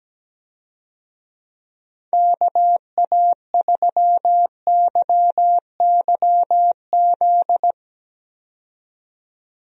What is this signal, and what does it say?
Morse code, heard as a keyed tone, about 17 wpm, 700 Hz. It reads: KA3YYZ
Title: USA Call Signs - 6 Characters - 17wpm